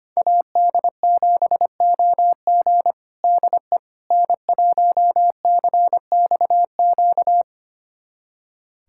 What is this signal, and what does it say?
Morse code, heard as a keyed tone, about 25 wpm, 700 Hz. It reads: AD7OG DE N1CXQ
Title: USA Callsign DE Callsign - 5 Characters - 25wpm